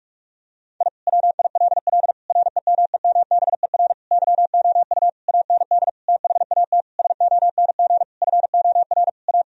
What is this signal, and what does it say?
Morse code, heard as a keyed tone, about 41 wpm, 700 Hz. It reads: I WILL REMEMBER YOU AND THAT SONG FOR A
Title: Sentences from Top 1000 Words - Review Part 2 - 45wpm